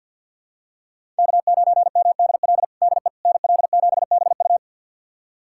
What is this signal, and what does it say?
Morse code, heard as a keyed tone, about 52 wpm, 700 Hz. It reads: K9MDL DE NL7BU
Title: USA Callsign DE Callsign - 5 Characters - 50wpm